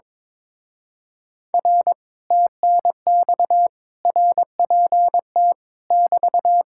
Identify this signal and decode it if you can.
Morse code, heard as a keyed tone, about 22 wpm, 700 Hz. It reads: R TNX RPT <BT>
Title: Speed-Racing Common QSO Phrases - 22wpm